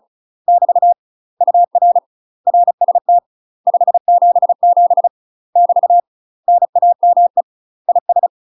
Morse code, heard as a keyed tone, about 36 wpm, 700 Hz. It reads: <BT> UR RST 577 <BT> NAME IS
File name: UK MASTER QSOs TRAINING / 35WPM / UK MASTER QSOs 0041